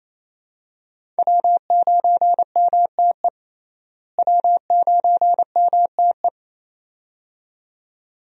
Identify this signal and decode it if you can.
Morse code, heard as a keyed tone, about 27 wpm, 700 Hz. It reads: W9MTE W9MTE
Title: USA Call Signs - 5 Characters - Repeated 2x - 28wpm